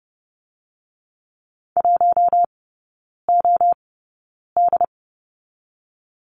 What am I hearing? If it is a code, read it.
Morse code, heard as a keyed tone, about 30 wpm, 700 Hz. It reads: 1 O D
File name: Lesson 19e - Letter w - Three Characters 30wpm